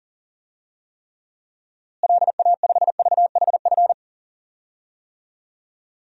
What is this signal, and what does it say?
Morse code, heard as a keyed tone, about 37 wpm, 700 Hz. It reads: LA5VHF
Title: DXCC-100 Call Signs - 6 Characters - Repeated 3x - 40wpm